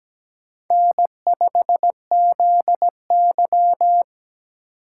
Morse code, heard as a keyed tone, about 17 wpm, 700 Hz. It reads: N5ZY
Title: USA Call Signs - 4 Characters - 3x Character Spacing - 17wpm